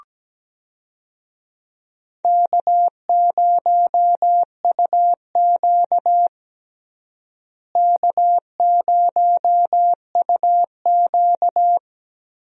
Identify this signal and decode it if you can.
Morse code, heard as a keyed tone, about 17 wpm, 700 Hz. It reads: K0UQ K0UQ
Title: USA Call Signs - 4 Characters - Repeated 3x - 17wpm